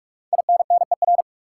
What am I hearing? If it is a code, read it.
Morse code, heard as a keyed tone, about 46 wpm, 700 Hz. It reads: INNER
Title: Five Letter Words - 45wpm